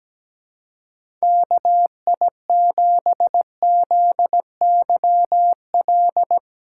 Morse code, heard as a keyed tone, about 17 wpm, 700 Hz. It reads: KI7ZYL
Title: USA Call Signs - 6 Characters - 4x Character Spacing - 17wpm